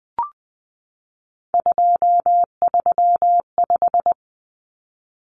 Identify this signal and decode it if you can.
Morse code, heard as a keyed tone, about 20 wpm, 700 Hz. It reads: 235
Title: Common Cut Numbers - 3-Digit 20wpm